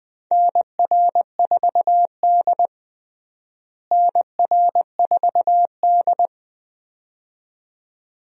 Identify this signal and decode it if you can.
Morse code, heard as a keyed tone, about 20 wpm, 700 Hz. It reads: NR4D NR4D
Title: USA Call Signs - 4 Characters - Repeated 2x - 20wpm